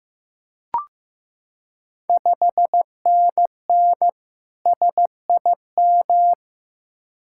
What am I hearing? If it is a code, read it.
Morse code, heard as a keyed tone, about 15 wpm, 700 Hz. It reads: 5NN SIM
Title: Ontario QSO Party Exchange 15wpm